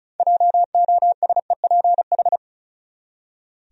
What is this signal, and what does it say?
Morse code, heard as a keyed tone, about 36 wpm, 700 Hz. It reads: JOSEPH
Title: USA Common Names - 35wpm